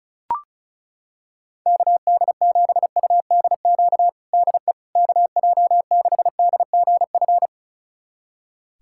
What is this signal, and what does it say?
Morse code, heard as a keyed tone, about 34 wpm, 700 Hz. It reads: KD7UDQ DE KJ6DGF
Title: USA Callsign DE Callsign - 6 Characters - 35wpm